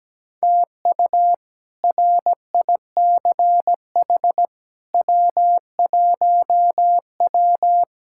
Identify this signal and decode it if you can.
Morse code, heard as a keyed tone, about 17 wpm, 700 Hz. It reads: TU RICH W1W
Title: K1USN SST QSO Practice - 17wpm